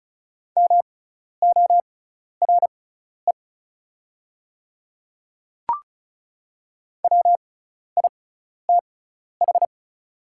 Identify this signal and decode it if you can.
Morse code, heard as a keyed tone, about 34 wpm, spaced as for 12 wpm, 700 Hz. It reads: MORE WITH
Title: 6x Character Spacing - Top 200 Words - 35wpm